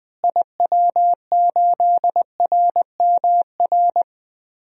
Morse code, heard as a keyed tone, about 21 wpm, 700 Hz. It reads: IW8RMR
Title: DXCC-100 Call Signs - 6 Characters - Repeated 3x - 20wpm